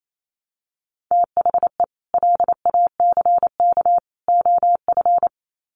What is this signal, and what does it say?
Morse code, heard as a keyed tone, about 28 wpm, 700 Hz. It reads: THE LACK OF
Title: Sets of 3 Words 28wpm